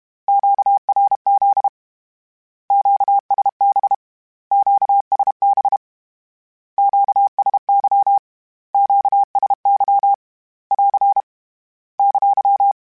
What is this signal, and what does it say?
Morse code, heard as a keyed tone, about 32 wpm, 800 Hz. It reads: QRZ QSB QSB QSY QSY <AR> !
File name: Q Calls / Q Calls-mp3s / Q Calls Twice @30